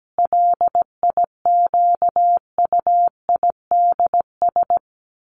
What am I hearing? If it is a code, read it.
Morse code, heard as a keyed tone, about 17 wpm, 700 Hz. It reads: LIQUIDS
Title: A Sound Of Thunder 17wpm